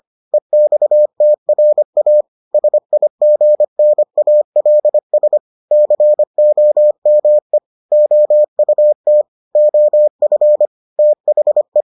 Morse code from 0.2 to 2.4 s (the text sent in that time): EXTRA